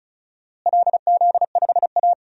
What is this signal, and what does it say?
Morse code, heard as a keyed tone, about 34 wpm, 700 Hz. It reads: LZ5A
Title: DXCC-100 Call Signs - 4 Characters - Repeated 2x - 35wpm